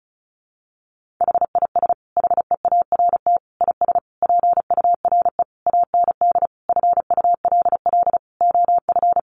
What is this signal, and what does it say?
Morse code, heard as a keyed tone, about 34 wpm, 700 Hz. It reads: HIS HEART IS PURE AND FULL OF
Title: Sentences from Top 1500 Words 35wpm